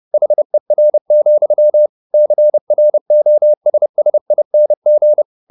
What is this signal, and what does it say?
Morse code, heard as a keyed tone, about 30 wpm, 600 Hz. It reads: HER, CROSSING
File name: the old man and the sea / the old man and the sea 1 30wpm